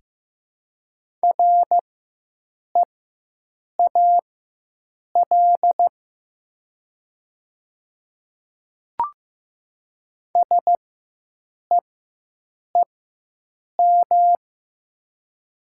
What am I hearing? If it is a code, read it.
Morse code, heard as a keyed tone, about 15 wpm, spaced as for 7 wpm, 700 Hz. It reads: REAL SEEM
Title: 4x Character Spacing - Top 300 Words - 15wpm